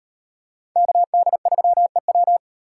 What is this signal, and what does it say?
Morse code, heard as a keyed tone, about 37 wpm, 700 Hz. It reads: KD3EW